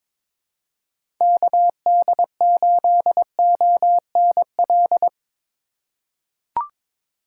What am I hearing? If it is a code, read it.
Morse code, heard as a keyed tone, about 22 wpm, 700 Hz. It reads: KD8ONL
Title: USA Call Signs - 6 Characters - 3x Character Spacing - 22wpm